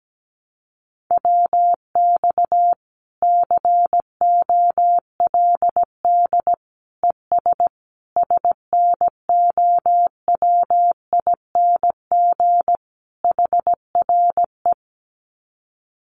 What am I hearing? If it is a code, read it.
Morse code, heard as a keyed tone, about 17 wpm, 700 Hz. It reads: WX COLD ES SNOWING HRE
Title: Common QSO Phrases 17wpm